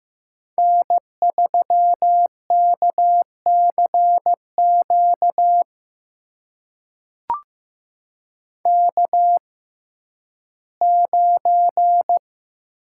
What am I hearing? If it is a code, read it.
Morse code, heard as a keyed tone, about 16 wpm, spaced as for 11 wpm, 700 Hz. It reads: N3KCQ K9
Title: USA Call Signs - 5 Characters - 6x Character Spacing - 15wpm